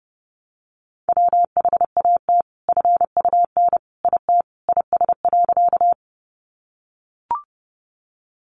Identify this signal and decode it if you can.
Morse code, heard as a keyed tone, about 30 wpm, 700 Hz. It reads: WHAT FUN IT IS.
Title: Sentences from Top 700 Words 30wpm